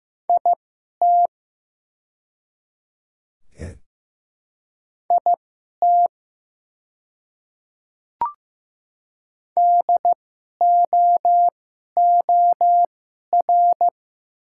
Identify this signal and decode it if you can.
Morse code, heard as a keyed tone, about 15 wpm, spaced as for 11 wpm, 700 Hz. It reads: IT IT DOOR
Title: 2x Character Spacing - Top 300 Words - 15wpm